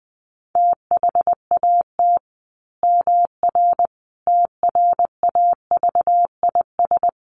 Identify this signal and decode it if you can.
Morse code, heard as a keyed tone, about 20 wpm, 700 Hz. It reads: THAT MR TRAVIS
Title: A Sound Of Thunder 20wpm